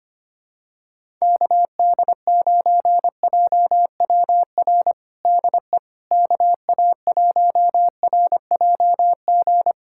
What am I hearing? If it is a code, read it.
Morse code, heard as a keyed tone, about 25 wpm, 700 Hz. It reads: KD9JWR DE KA1RJG
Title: USA Callsign DE Callsign 25wpm